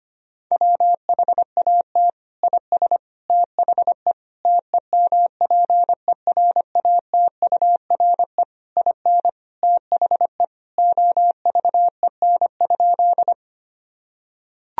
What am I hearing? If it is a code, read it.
Morse code, heard as a keyed tone, about 25 wpm, 700 Hz. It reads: WHAT IS THE TEMPERATURE IN THE OVEN?
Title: Sentences from Top 1000 Words - Review Part 4 - 25wpm